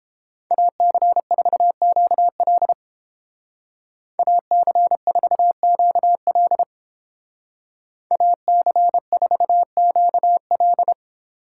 Morse code, heard as a keyed tone, about 30 wpm, 700 Hz. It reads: AC4QL AC4QL AC4QL